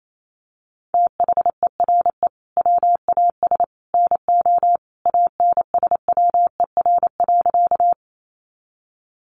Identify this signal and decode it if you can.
Morse code, heard as a keyed tone, about 28 wpm, 700 Hz. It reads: THERE WAS NO ANSWER.
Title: Sentences from Top 300 Words 28wpm